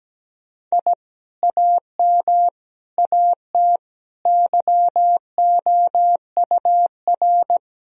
Morse code, heard as a keyed tone, about 17 wpm, 700 Hz. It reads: I AM AT YOUR
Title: Sentences from Top 1000 Words - Review Part 4 - 17wpm